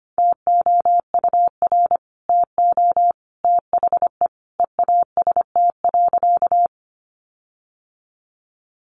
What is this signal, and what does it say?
Morse code, heard as a keyed tone, about 25 wpm, 700 Hz. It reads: TOUR TO THE EAST.
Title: Sentences from Top 1400 Words 25wpm